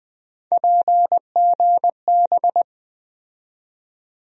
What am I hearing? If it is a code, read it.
Morse code, heard as a keyed tone, about 20 wpm, 700 Hz. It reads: PGB
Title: USA Call Sign Postfixes - 3 Letters 20wpm